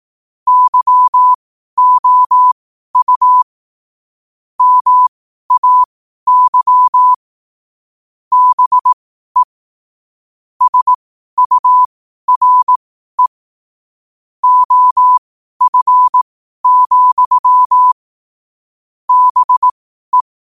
Morse code, heard as a keyed tone, about 18 wpm, spaced as for 12 wpm, 1000 Hz. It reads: YOU MAY BE SURE OF, BE
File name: qotdfiles / 12wpm